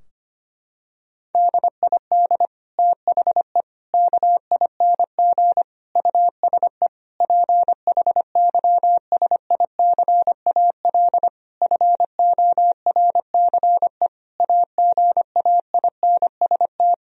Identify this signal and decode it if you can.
Morse code, heard as a keyed tone, about 25 wpm, 700 Hz. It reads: DID THE KING USE PHYSICAL FORCE AGAINST
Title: Sentences from Top 1000 Words - Review Part 4 - 25wpm